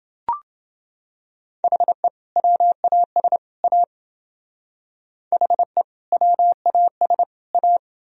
Morse code, heard as a keyed tone, about 28 wpm, 700 Hz. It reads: HE WAS A HE WAS A